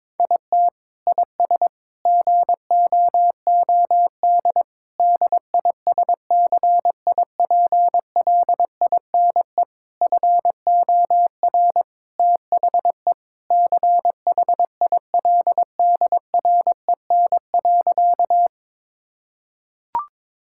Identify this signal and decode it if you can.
Morse code, heard as a keyed tone, about 22 wpm, 700 Hz. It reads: IT IS GOOD DISCIPLINE FOR THE CHILDREN.
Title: Sentences from Top 1000 Words - Review Part 6 - 22wpm